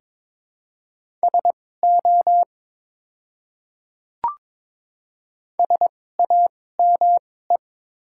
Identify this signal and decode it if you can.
Morse code, heard as a keyed tone, about 22 wpm, spaced as for 16 wpm, 700 Hz. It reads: SO SAME
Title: 2x Character Spacing - Top 200 Words - 22wpm